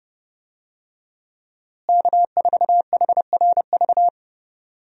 Morse code, heard as a keyed tone, about 30 wpm, 700 Hz. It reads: K4HRV
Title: USA Call Signs - 5 Characters - Repeated 2x - 30wpm